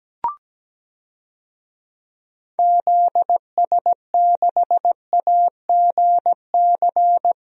Morse code, heard as a keyed tone, about 17 wpm, 700 Hz. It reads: ZS6AGC
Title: DXCC-100 Call Signs - 6 Characters - 17wpm